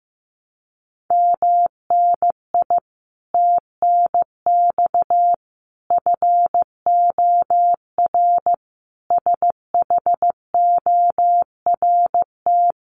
Morse code, heard as a keyed tone, about 15 wpm, 700 Hz. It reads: MNI TNX FOR SHORT
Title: Common QSO Phrases 15wpm